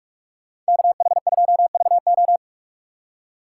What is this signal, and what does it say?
Morse code, heard as a keyed tone, about 46 wpm, 700 Hz. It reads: KH2VO
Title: USA Call Signs - 5 Characters - 3x Character Spacing - 45wpm